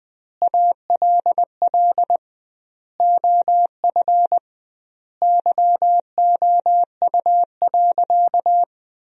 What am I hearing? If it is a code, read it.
Morse code, heard as a keyed tone, about 20 wpm, 700 Hz. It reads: ALL OF YOU.
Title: 2x Word-Spacing - Sentences from Top 500 Words - 20wpm